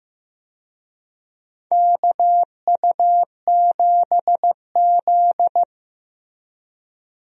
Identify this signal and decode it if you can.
Morse code, heard as a keyed tone, about 15 wpm, 700 Hz. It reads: KU7Z
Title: USA Call Signs - 4 Characters - Repeated 3x - 15wpm